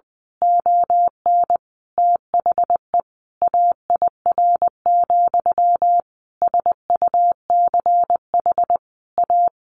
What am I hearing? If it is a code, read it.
Morse code, heard as a keyed tone, about 20 wpm, 700 Hz. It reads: ON THE AIR, SUCH A
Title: A Sound Of Thunder 20wpm